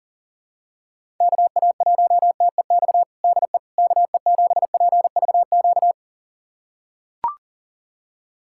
Morse code, heard as a keyed tone, about 37 wpm, 700 Hz. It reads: KA1TEX DE KE7PVQ
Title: USA Callsign DE Callsign 40wpm